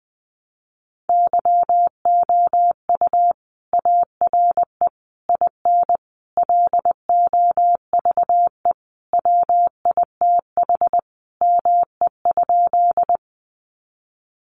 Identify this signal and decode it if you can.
Morse code, heard as a keyed tone, about 20 wpm, 700 Hz. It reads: YOU ARE IN LOVE WITH ME?
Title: Sentences from Top 400 Words 20wpm